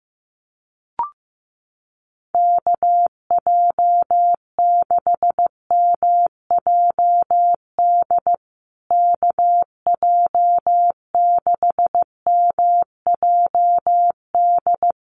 Morse code, heard as a keyed tone, about 15 wpm, 700 Hz. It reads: KJ6MJD KJ6MJD
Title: USA Call Signs Repeated 3x 15wpm